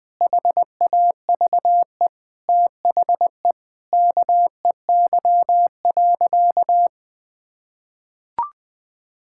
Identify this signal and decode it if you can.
Morse code, heard as a keyed tone, about 19 wpm, 700 Hz. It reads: HAVE THE KEY.
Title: Sentences with 4-Word Phrases - Spoken Beginning - 20wpm